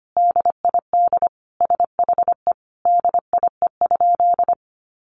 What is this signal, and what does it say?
Morse code, heard as a keyed tone, about 25 wpm, 700 Hz. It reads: DID SHE DIE?
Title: Sentences from Top 1300 Words 25wpm